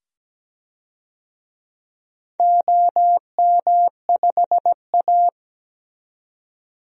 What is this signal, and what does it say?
Morse code, heard as a keyed tone, about 17 wpm, 700 Hz. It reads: OM5A
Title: DXCC-100 Call Signs - 4 Characters - 17wpm